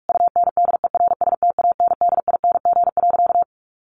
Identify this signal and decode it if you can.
Morse code, heard as a keyed tone, about 44 wpm, 700 Hz. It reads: UNDERSTANDING.
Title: Sentences from Top 1400 Words 45wpm